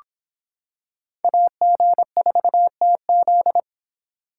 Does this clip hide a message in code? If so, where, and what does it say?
Yes: Morse code from 1.2 to 3.6 s, heard as a keyed tone, about 25 wpm, 700 Hz. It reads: AG4TZ